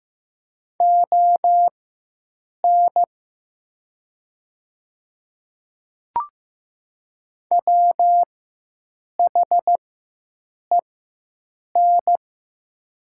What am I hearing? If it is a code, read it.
Morse code, heard as a keyed tone, about 15 wpm, spaced as for 7 wpm, 700 Hz. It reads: ON WHEN
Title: 4x Character Spacing - Top 100 Words - 15wpm